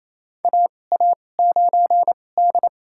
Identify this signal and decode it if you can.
Morse code, heard as a keyed tone, about 28 wpm, spaced as for 20 wpm, 700 Hz. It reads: AA9D
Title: USA Call Signs - 4 Characters - 2x Character Spacing - 28wpm